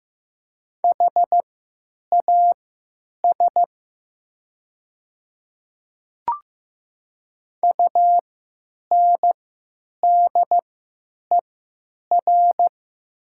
Morse code, heard as a keyed tone, about 15 wpm, spaced as for 9 wpm, 700 Hz. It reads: HAS UNDER
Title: 3x Character Spacing - Top 200 Words - 15wpm